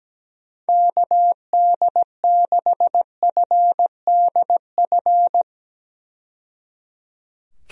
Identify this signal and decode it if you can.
Morse code, heard as a keyed tone, about 17 wpm, 700 Hz. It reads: KD6FDF
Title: USA Call Signs - 6 Characters - 17wpm